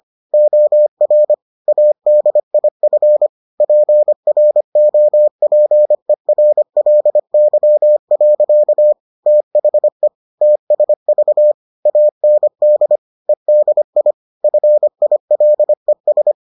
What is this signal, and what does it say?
Morse code, heard as a keyed tone, about 25 wpm, 600 Hz. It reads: OR ADIF PROPERLY. THE TSV AND EDI FILES